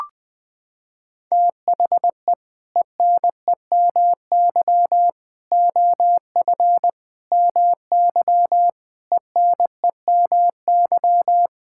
Morse code, heard as a keyed tone, about 21 wpm, 700 Hz. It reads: THE ENEMY OF MY ENEMY
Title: Sets of 8 Words - 20wpm